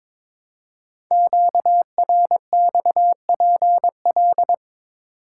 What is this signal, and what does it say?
Morse code, heard as a keyed tone, about 22 wpm, 700 Hz. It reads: QRXPL
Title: Code Groups - 5 Letters - 22wpm